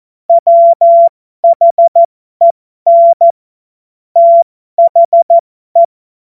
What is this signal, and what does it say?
Morse code, heard as a keyed tone, about 14 wpm, spaced as for 12 wpm, 680 Hz. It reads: WHEN THE
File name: Morsekurs / Witze